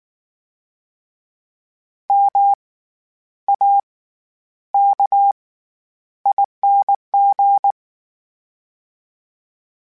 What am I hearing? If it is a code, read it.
Morse code, heard as a keyed tone, about 19 wpm, spaced as for 10 wpm, 800 Hz. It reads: MAKING
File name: Special INGs / Special INGs-mp3s / SlowToFaster ing @18